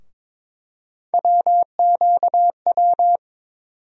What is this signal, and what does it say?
Morse code, heard as a keyed tone, about 22 wpm, 700 Hz. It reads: WQW
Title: USA Call Sign Postfixes - 3 Letters 22wpm